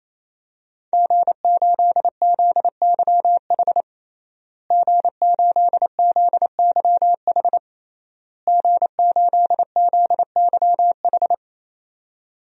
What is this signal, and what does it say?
Morse code, heard as a keyed tone, about 27 wpm, 700 Hz. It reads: G8ZYH G8ZYH G8ZYH
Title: DXCC-100 Call Signs - 5 Characters - Repeated 3x - 28wpm